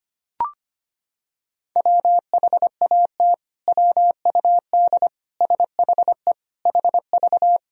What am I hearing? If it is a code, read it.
Morse code, heard as a keyed tone, about 25 wpm, 700 Hz. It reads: WHAT WUD SHE HV
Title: Sentences from Top 100 Words and Abbreviations 25wpm